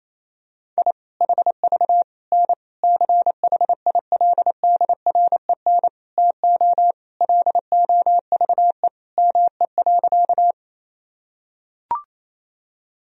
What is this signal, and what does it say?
Morse code, heard as a keyed tone, about 28 wpm, 700 Hz. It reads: I HV N CHILDREN TO LOVE ME.
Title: Sentences from Top 400 Words and Abbreviations 28wpm